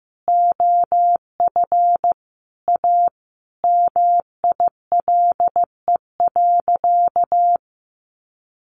Morse code, heard as a keyed tone, about 15 wpm, 700 Hz. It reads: OF A MILE.
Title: Sentences from Top 1300 Words 15wpm